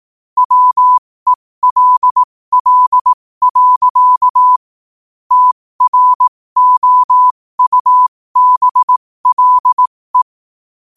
Morse code, heard as a keyed tone, about 18 wpm, spaced as for 15 wpm, 1000 Hz. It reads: WELL. TROUBLE
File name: qotdfiles / 15wpm